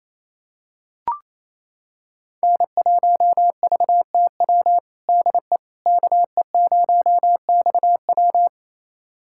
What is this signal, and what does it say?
Morse code, heard as a keyed tone, about 27 wpm, 700 Hz. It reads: N1VTW DE KE0XW
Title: USA Callsign DE Callsign - 5 Characters - 28wpm